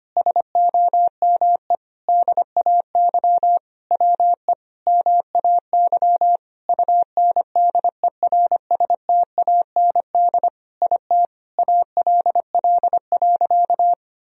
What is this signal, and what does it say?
Morse code, heard as a keyed tone, about 25 wpm, 700 Hz. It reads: SOME DAY WE MAY UNDERSTAND IT ALL.
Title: Sentences from Top 1000 Words - Review Part 4 - 25wpm